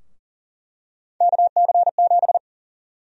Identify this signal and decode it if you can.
Morse code, heard as a keyed tone, about 40 wpm, 700 Hz. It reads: KC7
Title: USA Call Sign Prefixes - Repeated 2x - 40wpm